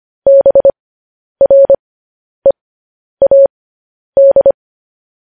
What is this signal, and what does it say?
Morse code, heard as a keyed tone, about 25 wpm, spaced as for 10 wpm, 550 Hz. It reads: BREAD
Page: news headlines in morse code at 10 wpm